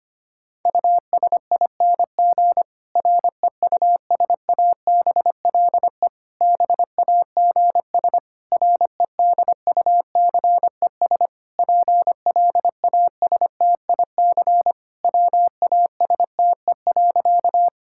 Morse code, heard as a keyed tone, about 25 wpm, 700 Hz. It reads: USING REUSABLE BAGS REDUCES PLASTIC WASTE.
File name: practice / 25wpm / Difficult Sentences with Hints - 25wpm